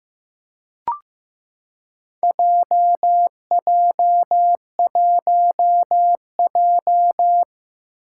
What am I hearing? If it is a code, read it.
Morse code, heard as a keyed tone, about 15 wpm, 700 Hz. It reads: JJ1J
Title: Four Challenging Letter-Number Combinations 15wpm